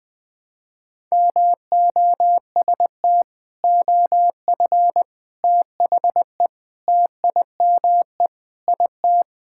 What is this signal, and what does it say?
Morse code, heard as a keyed tone, about 21 wpm, 700 Hz. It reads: MOST OF THE TIME IT
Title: Sentences with 4-Word Phrases - Spoken Beginning - 20wpm